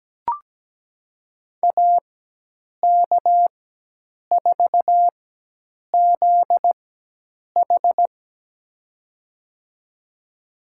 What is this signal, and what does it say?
Morse code, heard as a keyed tone, about 17 wpm, spaced as for 8 wpm, 700 Hz. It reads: AK4ZH
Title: USA Call Signs - 5 Characters - 4x Character Spacing - 17wpm